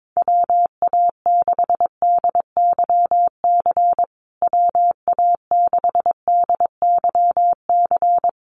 Morse code, heard as a keyed tone, about 22 wpm, 700 Hz. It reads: WA6DYC WA6DYC
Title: USA Call Signs Repeated 3x 22wpm